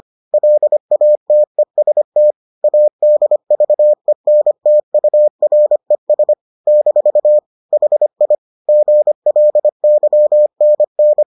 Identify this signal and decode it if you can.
Morse code, heard as a keyed tone, about 25 wpm, 600 Hz. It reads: LATEST ADVENTURES - HI GLYNN